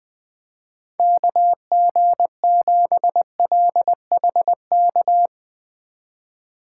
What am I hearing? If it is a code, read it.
Morse code, heard as a keyed tone, about 21 wpm, 700 Hz. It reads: KG7LHK
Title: Speed Racing - USA Call Signs 17wpm